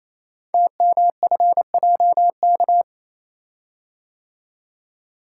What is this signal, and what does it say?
Morse code, heard as a keyed tone, about 29 wpm, 700 Hz. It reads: TMFJK
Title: Code Groups - 5 Letter:Numbers - 28wpm